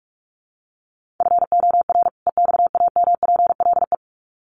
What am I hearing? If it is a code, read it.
Morse code, heard as a keyed tone, about 44 wpm, 700 Hz. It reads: FOR EXAMPLE
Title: Sets of 2 Words 45wpm